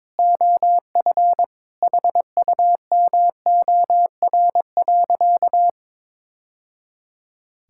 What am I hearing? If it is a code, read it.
Morse code, heard as a keyed tone, about 22 wpm, 700 Hz. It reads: OF HUMOR.
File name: practice / 22wpm / Sentences with 2-Word Phrases - 22wpm